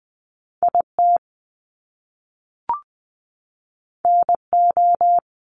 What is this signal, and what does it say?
Morse code, heard as a keyed tone, about 20 wpm, 700 Hz. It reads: IT NO
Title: Top Two Letter Combinations in Words 20wpm